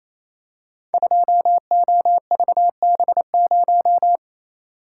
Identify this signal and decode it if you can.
Morse code, heard as a keyed tone, about 27 wpm, 700 Hz. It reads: 2OVB0
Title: Code Groups - 5 Letter:Numbers - 28wpm